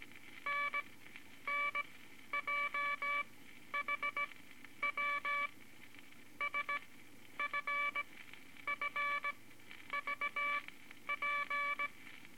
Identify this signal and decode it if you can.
Morse code, heard as a keyed tone, about 18 wpm, spaced as for 10 wpm, 2300 Hz, with noise. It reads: NNJHW SFFVP
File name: Morse Code Records / 18 B LCM 165 Ten Words Per Minute